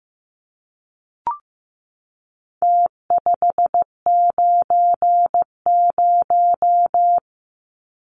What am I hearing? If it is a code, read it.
Morse code, heard as a keyed tone, about 15 wpm, 700 Hz. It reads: T590
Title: Numbers - Contest Style - 4-Digit 15wpm